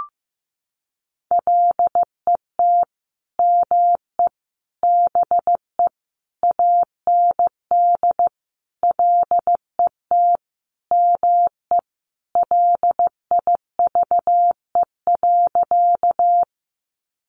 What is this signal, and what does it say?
Morse code, heard as a keyed tone, about 14 wpm, 700 Hz. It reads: LET ME BE AND LET ME LIVE.
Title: Sentences from Top 300 Words 15wpm